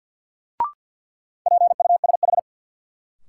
Morse code, heard as a keyed tone, about 53 wpm, 700 Hz. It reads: PUSH
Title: Sentences from Top 1300 Words - New Word Drill Alt - 50wpm